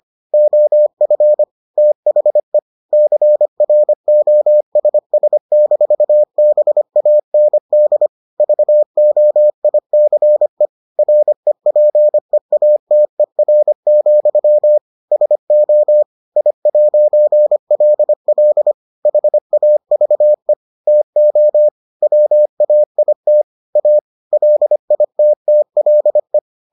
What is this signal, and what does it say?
Morse code, heard as a keyed tone, about 25 wpm, 600 Hz. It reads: OF THE CROSS-BAND VOICE REPEATER, SO I'LL HAVE TO WAIT A LITTLE